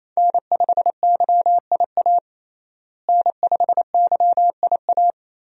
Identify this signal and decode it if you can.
Morse code, heard as a keyed tone, about 28 wpm, 700 Hz. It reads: N5YIA N5YIA
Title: USA Call Signs - 5 Characters - Repeated 3x - 28wpm